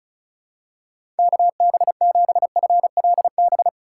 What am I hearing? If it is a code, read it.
Morse code, heard as a keyed tone, about 35 wpm, 690 Hz. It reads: KB7FLB
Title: USA Call Signs - 3x Character Spacing - 35wpm